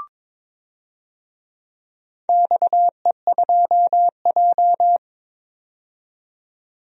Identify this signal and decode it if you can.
Morse code, heard as a keyed tone, about 22 wpm, 700 Hz. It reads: XE2J
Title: DXCC-100 Call Signs - 4 Characters - 22wpm